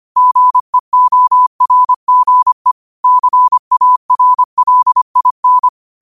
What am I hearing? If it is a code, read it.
Morse code, heard as a keyed tone, about 25 wpm, 1000 Hz. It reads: GEORGE CARLIN
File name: qotdfiles / 25wpm